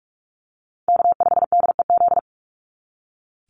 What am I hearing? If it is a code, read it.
Morse code, heard as a keyed tone, about 44 wpm, 700 Hz. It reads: K5DEZ
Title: NA SOTA Chaser Callsigns 45wpm